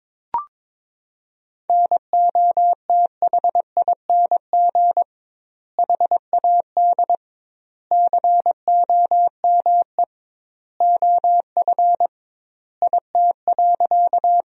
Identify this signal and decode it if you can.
Morse code, heard as a keyed tone, about 22 wpm, 700 Hz. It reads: NOTHING HAD COME OF IT.
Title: 2x Word-Spacing - Sentences from Top 500 Words - 22wpm